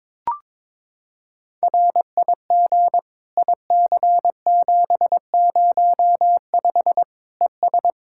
Morse code, heard as a keyed tone, about 22 wpm, 700 Hz. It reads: RIG IC705 ES
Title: Practice QSO - Exchange 2 - Info - 22wpm